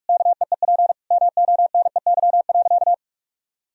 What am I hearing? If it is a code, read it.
Morse code, heard as a keyed tone, about 41 wpm, 700 Hz. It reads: KEEP MONEY.
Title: Sentences from Top 1000 Words - Review Part 4 - 45wpm